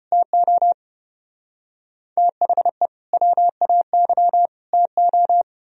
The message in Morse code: TO THE WAY TO

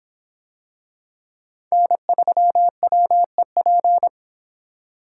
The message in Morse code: N3WEP

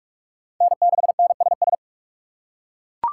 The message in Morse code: N6NSS

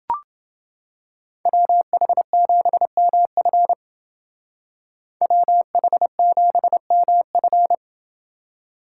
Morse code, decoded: WH7MF WH7MF

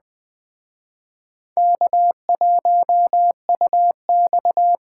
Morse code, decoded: K1UX

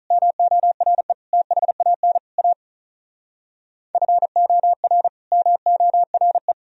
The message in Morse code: MORE THAN A FOR MORE

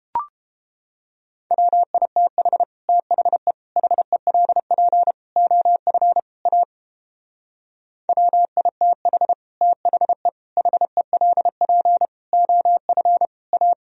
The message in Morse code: WITH THE HELP OF A WITH THE HELP OF A

